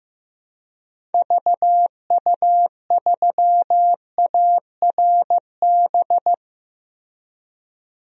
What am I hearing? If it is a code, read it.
Morse code, heard as a keyed tone, about 15 wpm, 700 Hz. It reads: VU3ARB
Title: DXCC-100 Call Signs - 6 Characters - Repeated 2x - 15wpm